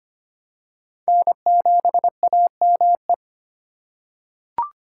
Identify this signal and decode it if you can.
Morse code, heard as a keyed tone, about 25 wpm, 700 Hz. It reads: N7AME
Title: K1USN SST QSO Practice - 25wpm